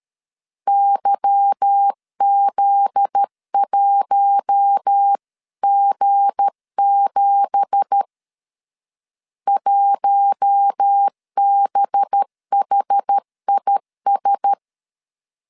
Morse code, heard as a keyed tone, about 13 wpm, 800 Hz. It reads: YZ1G7 1BHIS
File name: CW / DET / 1 9 1500C S A 12wpm 3800